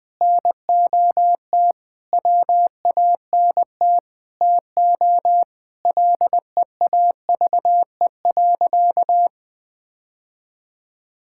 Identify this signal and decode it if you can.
Morse code, heard as a keyed tone, about 20 wpm, 700 Hz. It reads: NOT WANT TO LEAVE.
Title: Sentences with 3-Word Phrases - 20wpm